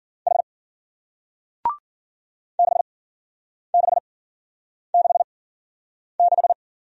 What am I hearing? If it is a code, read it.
Morse code, heard as a keyed tone, about 52 wpm, spaced as for 9 wpm, 700 Hz. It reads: H 6666